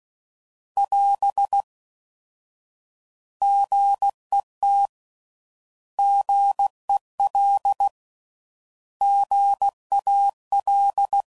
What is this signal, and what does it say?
Morse code, heard as a keyed tone, about 16 wpm, 800 Hz. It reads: <AS> GET GEL GAL